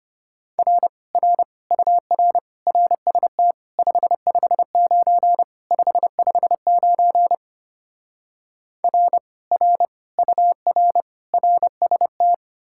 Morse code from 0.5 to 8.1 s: R R UR RST 559 559

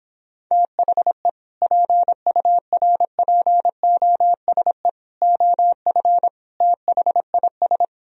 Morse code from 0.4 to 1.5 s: THE

